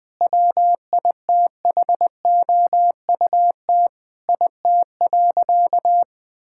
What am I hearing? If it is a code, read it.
Morse code, heard as a keyed tone, about 20 wpm, 700 Hz. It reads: WITHOUT IT.
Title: Sentences with 5-Word Phrases - Spoken Beginning - 20wpm